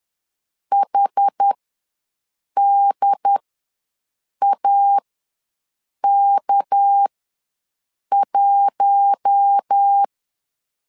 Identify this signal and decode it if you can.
Morse code, heard as a keyed tone, about 10 wpm, spaced as for 6 wpm, 800 Hz. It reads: HDAK1
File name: LEZIONI / 6 PKX0 / 6 PKX ws 10wpm 3706